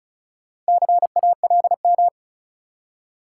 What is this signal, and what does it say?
Morse code, heard as a keyed tone, about 36 wpm, 700 Hz. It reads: CALM
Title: Four Letter Words - 35wpm